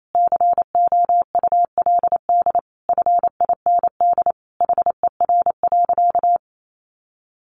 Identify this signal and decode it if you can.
Morse code, heard as a keyed tone, about 28 wpm, 700 Hz. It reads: COULD FIND HER.
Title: Sentences from Top 1400 Words 28wpm